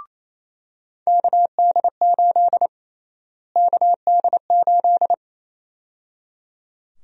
Morse code, heard as a keyed tone, about 28 wpm, 700 Hz. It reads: KD8 KD8
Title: USA Call Sign Prefixes - Repeated 2x - 28wpm